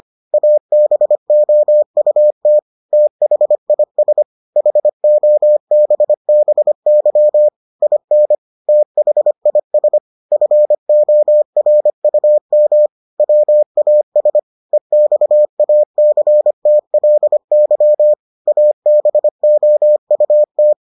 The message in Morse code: ABOUT THIS HOBBY IN THIS FORUM WAS EXACTLY ABOUT

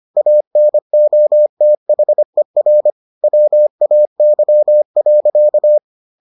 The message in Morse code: ANOTHER WAY.

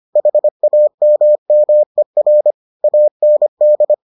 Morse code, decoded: HAMMER AND

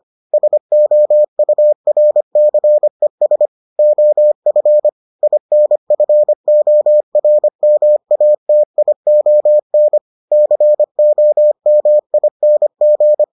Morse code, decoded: SOURCES OF INFORMATION COMING

